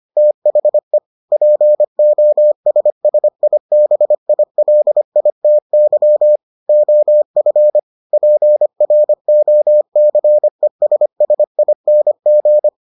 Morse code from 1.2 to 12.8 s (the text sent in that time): POSSIBILITY OF PROCESSING